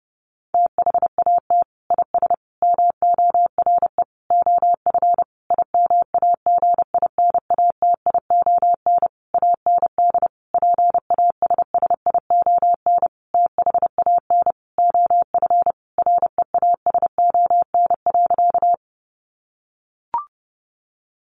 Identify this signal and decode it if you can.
Morse code, heard as a keyed tone, about 30 wpm, 700 Hz. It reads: THAT IS MORE OF IMAGINATION AND PASSION THAN OF REASON.
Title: Sentences from Top 1300 Words 30wpm